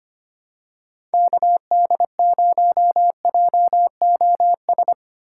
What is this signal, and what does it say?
Morse code, heard as a keyed tone, about 25 wpm, 700 Hz. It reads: KD0JOS
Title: USA Call Signs - 4x Character Spacing - 25wpm